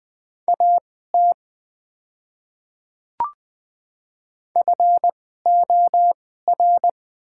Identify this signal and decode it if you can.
Morse code, heard as a keyed tone, about 21 wpm, spaced as for 15 wpm, 690 Hz. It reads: AT FOR
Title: 2x Character Spacing - Top 500 Words - 20wpm